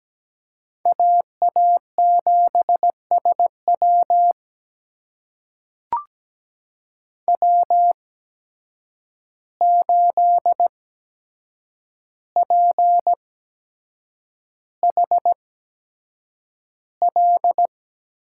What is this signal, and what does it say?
Morse code, heard as a keyed tone, about 17 wpm, spaced as for 7 wpm, 700 Hz. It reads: AA7SW W8PHL
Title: USA Call Signs - 5 Characters - 8x Character Spacing - 17wpm